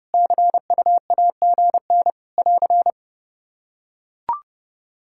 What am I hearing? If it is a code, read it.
Morse code, heard as a keyed tone, about 32 wpm, 700 Hz. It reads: CUAGN <AR>
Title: Practice QSO - Exchange 4 - Ending - 30wpm